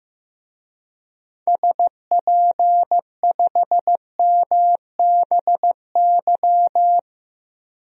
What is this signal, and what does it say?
Morse code, heard as a keyed tone, about 15 wpm, 700 Hz. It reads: SP5MBY
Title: DXCC-100 Call Signs - 6 Characters - 15wpm